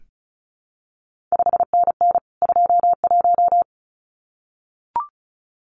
Morse code, heard as a keyed tone, about 36 wpm, 700 Hz. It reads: 5NN 21
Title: CQ WW DX Contest Exchange Encore - 35wpm